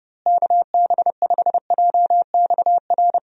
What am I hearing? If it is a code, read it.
Morse code, heard as a keyed tone, about 30 wpm, 700 Hz. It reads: KB5JXR
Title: USA Call Signs - 6 Characters - Repeated 3x - 30wpm